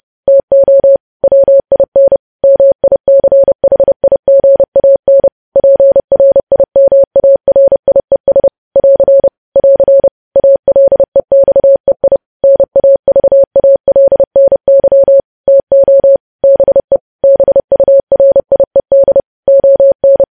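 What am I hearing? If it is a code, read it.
Morse code, heard as a keyed tone, about 30 wpm, 550 Hz. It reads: TO WIN MICHIGAN PRIMARIES <AR> <AR> ALEXEI NAVALNY TO BE BURIED ON
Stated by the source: News headlines sent in Morse code at 30.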